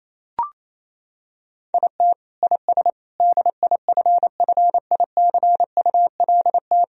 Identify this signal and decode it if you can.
Morse code, heard as a keyed tone, about 28 wpm, 700 Hz. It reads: IT IS DIFFICULT
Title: Sentences from Top 1000 Words - Review Part 3 - 28wpm